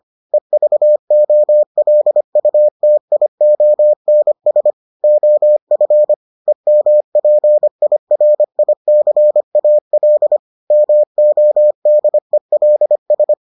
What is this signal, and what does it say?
Morse code, heard as a keyed tone, about 25 wpm, 600 Hz. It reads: EVOLUTIONS OF EMPIRICAL MODELS